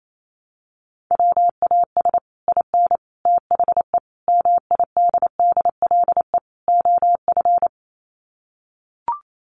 Morse code, heard as a keyed tone, about 28 wpm, 700 Hz. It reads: WAS IN THE MIDDLE OF
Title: Sets of 5 Words Encore 28wpm